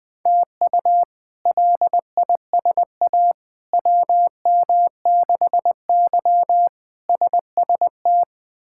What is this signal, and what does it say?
Morse code, heard as a keyed tone, about 20 wpm, 700 Hz. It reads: TU LISA WM6Y SST
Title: K1USN SST QSO Practice - 20wpm